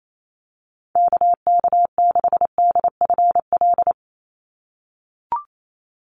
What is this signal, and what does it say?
Morse code, heard as a keyed tone, about 28 wpm, 700 Hz. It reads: KK6DFL
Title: USA Call Signs 28wpm